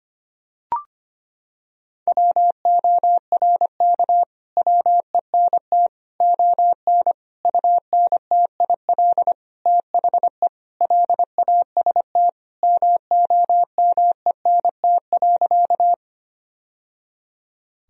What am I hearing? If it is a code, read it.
Morse code, heard as a keyed tone, about 25 wpm, 700 Hz. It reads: WORK WENT ON UNTIL THE LAST MOMENT.
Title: Sentences from Top 1000 Words - Review Part 1 - 25wpm